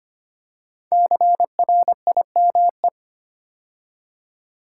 Morse code, heard as a keyed tone, about 25 wpm, 700 Hz. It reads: CRIME
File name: 5 Letter Words - First 3 Letters Given - 25wpm